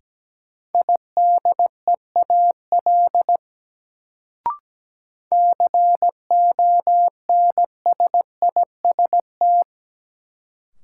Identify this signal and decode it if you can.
Morse code, heard as a keyed tone, about 17 wpm, 700 Hz. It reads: IDEAL CONSIST
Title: Sentences from Top 1400 Words - New Word Drill Alt - 17wpm